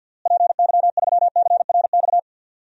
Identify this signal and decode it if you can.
Morse code, heard as a keyed tone, about 51 wpm, 700 Hz. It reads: PY3CRX
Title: DXCC-100 Call Signs - 6 Characters - Repeated 3x - 50wpm